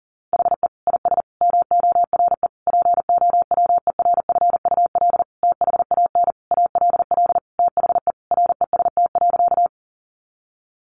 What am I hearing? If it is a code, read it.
Morse code, heard as a keyed tone, about 37 wpm, 700 Hz. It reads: HE IS MORE POWERFUL THAN ALL THE REST.
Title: Sentences from Top 1300 Words 40wpm